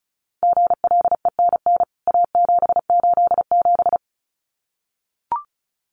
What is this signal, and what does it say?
Morse code, heard as a keyed tone, about 34 wpm, 700 Hz. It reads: GLENN A787
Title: CWT Exchange 35wpm